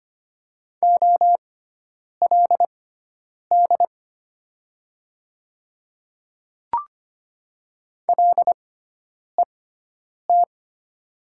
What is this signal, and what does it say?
Morse code, heard as a keyed tone, about 25 wpm, spaced as for 9 wpm, 700 Hz. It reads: OLD LET
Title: 6x Character Spacing - Top 500 Words - 25wpm